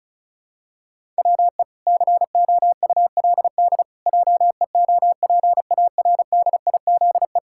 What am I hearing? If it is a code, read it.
Morse code, heard as a keyed tone, about 36 wpm, 700 Hz. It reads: WE COULD JEOPARDIZE
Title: Pangram Challenge v2 - 35wpm